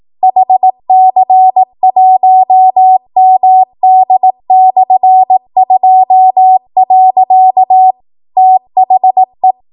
Morse code, heard as a keyed tone, about 18 wpm, 750 Hz. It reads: HC1MD/2. THE
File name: Archive / 18 WPM / 18WPM